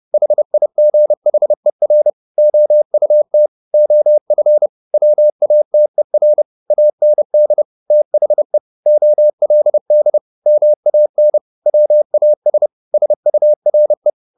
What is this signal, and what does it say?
Morse code, heard as a keyed tone, about 30 wpm, 600 Hz. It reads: HIGHER OUT OF WATER AND THE OLD MAN WAS SURE